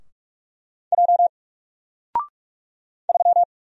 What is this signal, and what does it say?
Morse code, heard as a keyed tone, about 44 wpm, 700 Hz. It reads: J 3
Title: Single Letter-Number 45wpm